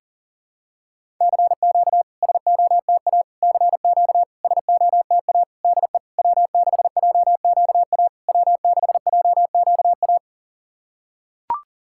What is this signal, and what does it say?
Morse code, heard as a keyed tone, about 40 wpm, 690 Hz. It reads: CQ SOTA CQ SOTA DE W6JQA W6JQA
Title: SOTA QSO Practice 40wpm